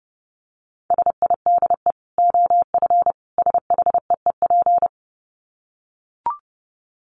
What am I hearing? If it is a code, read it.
Morse code, heard as a keyed tone, about 30 wpm, 700 Hz. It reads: SIDE OF SHEEP
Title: Dit Challenge 30wpm